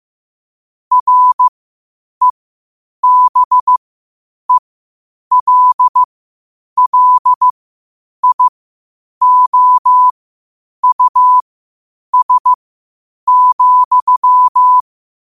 Morse code, heard as a keyed tone, about 15 wpm, spaced as for 8 wpm, 1000 Hz. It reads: REBELLIOUS,
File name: qotdfiles / 8wpm